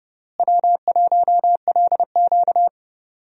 Morse code, heard as a keyed tone, about 30 wpm, 700 Hz. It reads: W1LQ
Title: USA Call Signs - 4 Characters - Repeated 3x - 30wpm